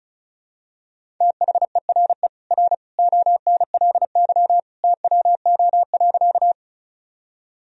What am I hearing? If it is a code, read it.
Morse code, heard as a keyed tone, about 36 wpm, 700 Hz. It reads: THERE R ONLY TWO.
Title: Sentences from Top 200 Words and Abbreviations 35wpm